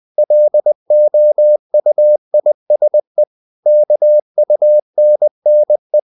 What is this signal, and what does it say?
Morse code, heard as a keyed tone, about 20 wpm, 600 Hz. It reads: LOUISE KUNNE
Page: Ferie 20 wpm